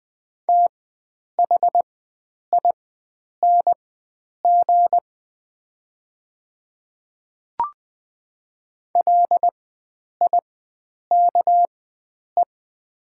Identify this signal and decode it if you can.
Morse code, heard as a keyed tone, about 20 wpm, spaced as for 9 wpm, 700 Hz. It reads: THING LIKE
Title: 4x Character Spacing - Top 100 Words - 20wpm